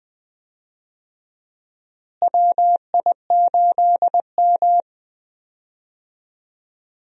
Morse code, heard as a keyed tone, about 20 wpm, 700 Hz. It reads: WI8M
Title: USA Call Signs - 4 Characters - Repeated 2x - 20wpm